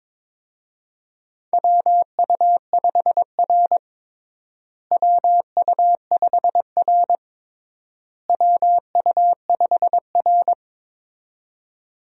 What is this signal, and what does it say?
Morse code, heard as a keyed tone, about 21 wpm, 700 Hz. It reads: WU5R WU5R WU5R
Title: USA Call Signs - 4 Characters - Repeated 3x - 22wpm